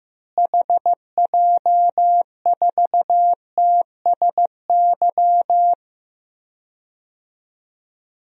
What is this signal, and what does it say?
Morse code, heard as a keyed tone, about 15 wpm, 700 Hz. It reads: HJ4TSY
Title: DXCC-100 Call Signs - 6 Characters - Repeated 3x - 15wpm